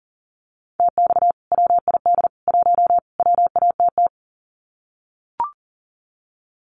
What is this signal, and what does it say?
Morse code, heard as a keyed tone, about 40 wpm, 700 Hz. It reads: TX WID 1 WATT
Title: Common QSO Phrases 40wpm